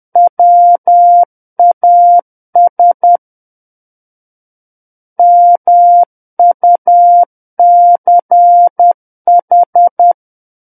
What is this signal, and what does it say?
Morse code, heard as a keyed tone, about 10 wpm, 700 Hz. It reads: WAS MUCH
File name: Book One / 01-Chapter 1 - 10 WPM